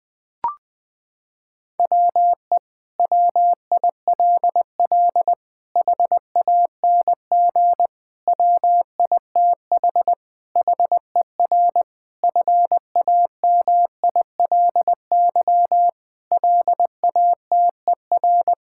Morse code, heard as a keyed tone, about 20 wpm, 700 Hz. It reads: WE WILL HANG WITH HER FAMILY LATER
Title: Sentences from Top 1600 Words 20wpm